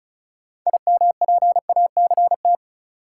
Morse code, heard as a keyed tone, about 36 wpm, 700 Hz. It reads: IMPACT
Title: Sentences from Top 1000 Words - Review Part 6 - New Word Drill - 35wpm